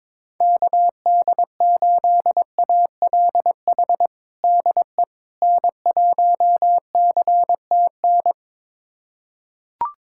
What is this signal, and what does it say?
Morse code, heard as a keyed tone, about 22 wpm, 700 Hz. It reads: KD8ALH DE N1CTN
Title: USA Callsign DE Callsign 22wpm